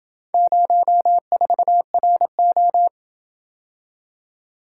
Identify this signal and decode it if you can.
Morse code, heard as a keyed tone, about 27 wpm, 700 Hz. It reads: 04RO